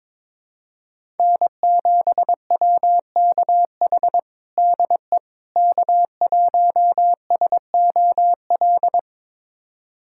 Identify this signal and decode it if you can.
Morse code, heard as a keyed tone, about 22 wpm, 700 Hz. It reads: N7WKH DE K1SOL
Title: USA Callsign DE Callsign - 5 Characters - 22wpm